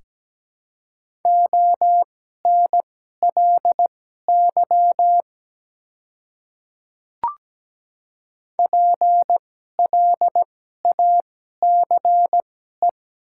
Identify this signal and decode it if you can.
Morse code, heard as a keyed tone, about 17 wpm, spaced as for 12 wpm, 700 Hz. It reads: ONLY PLACE
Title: 2x Character Spacing - Top 200 Words - 17wpm